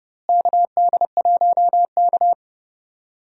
Morse code, heard as a keyed tone, about 31 wpm, 700 Hz. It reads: KD1K
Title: USA Call Signs - 4 Characters - Repeated 3x - 30wpm